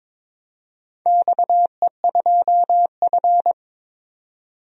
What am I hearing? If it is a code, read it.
Morse code, heard as a keyed tone, about 22 wpm, 700 Hz. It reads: XE2F
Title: DXCC-100 Call Signs - 4 Characters - 22wpm